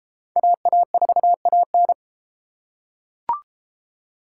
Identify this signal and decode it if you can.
Morse code, heard as a keyed tone, about 31 wpm, 700 Hz. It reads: AA4AN